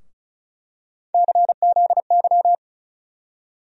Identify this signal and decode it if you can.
Morse code, heard as a keyed tone, about 36 wpm, 700 Hz. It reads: CZY
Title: USA Call Sign Postfixes - 3 Letters 35wpm